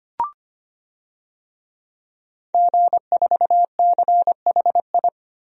Morse code, heard as a keyed tone, about 25 wpm, 700 Hz. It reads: G4CHI
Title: DXCC-100 Call Signs - 5 Characters - Repeated 2x - 25wpm